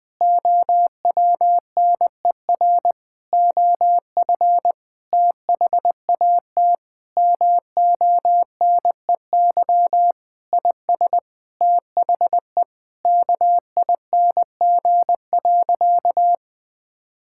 Morse code, heard as a keyed tone, about 21 wpm, 700 Hz. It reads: OWNER OF THAT MONEY IS THE KING.
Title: Sentences from Top 1600 Words 20wpm